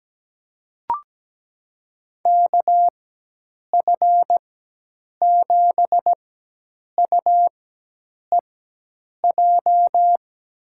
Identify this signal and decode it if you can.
Morse code, heard as a keyed tone, about 17 wpm, spaced as for 8 wpm, 700 Hz. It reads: KF7UEJ
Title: USA Call Signs - 6 Characters - 4x Character Spacing - 17wpm